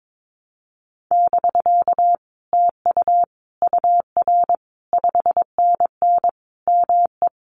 Morse code, heard as a keyed tone, about 22 wpm, 700 Hz. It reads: <BK> TU UR 5NN ME
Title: Common QSO Phrases Encore 22wpm